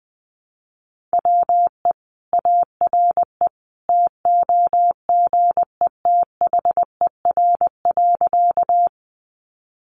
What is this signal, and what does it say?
Morse code, heard as a keyed tone, about 20 wpm, 700 Hz. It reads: WE ARE TOGETHER.
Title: Sentences from Top 300 Words 20wpm